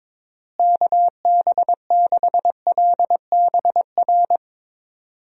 Morse code, heard as a keyed tone, about 21 wpm, 700 Hz. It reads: KB6LBR
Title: USA Call Signs - 6 Characters - Repeated 2x - 22wpm